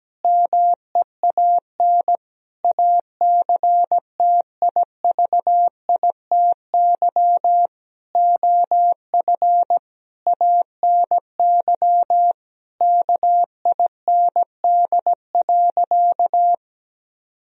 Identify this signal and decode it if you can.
Morse code, heard as a keyed tone, about 17 wpm, 700 Hz. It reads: MEAN ACTIVITY OF ANY KIND.
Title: Sentences from Top 1000 Words - Review Part 4 - 17wpm